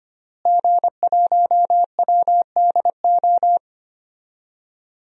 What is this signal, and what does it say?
Morse code, heard as a keyed tone, about 25 wpm, 700 Hz. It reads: G1WDO
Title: DXCC-100 Call Signs - 5 Characters - Repeated 2x - 25wpm